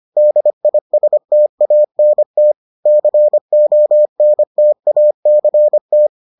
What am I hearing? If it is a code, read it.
Morse code, heard as a keyed tone, about 25 wpm, 600 Hz. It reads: DISTANT CONTACT